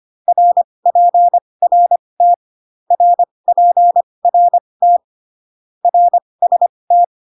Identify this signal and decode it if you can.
Morse code, heard as a keyed tone, about 25 wpm, spaced as for 20 wpm, 700 Hz. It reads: RPRT RPRT RST